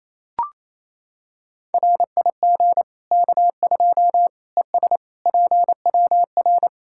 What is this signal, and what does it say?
Morse code, heard as a keyed tone, about 28 wpm, 700 Hz. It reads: RIG K2 ES PWR
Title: Practice QSO - Exchange 2 - Info - 28wpm